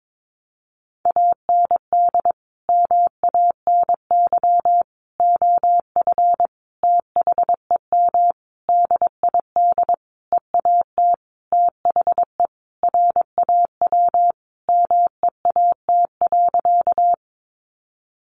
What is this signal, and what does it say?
Morse code, heard as a keyed tone, about 22 wpm, 700 Hz. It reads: AND MANY OF THEM DID EAT THE RAW MEAT.
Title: Sentences from Top 1500 Words 22wpm